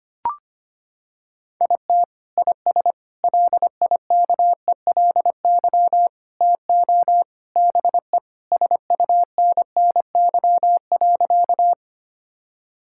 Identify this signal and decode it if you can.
Morse code, heard as a keyed tone, about 25 wpm, 700 Hz. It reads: IT IS LIKELY TO BE SUNNY.
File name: practice / 25wpm / Sentences with 3-Word Phrases - 25wpm